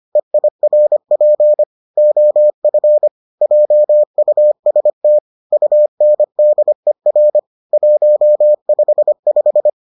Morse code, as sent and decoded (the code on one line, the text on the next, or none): . .. .-. .--. / --- ..-. / .--- ..- ... - / ..- -. -.. .-. / .---- ..... .....
EIRP OF JUST UNDER 155